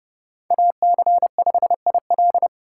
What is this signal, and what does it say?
Morse code, heard as a keyed tone, about 31 wpm, 700 Hz. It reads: AC5IL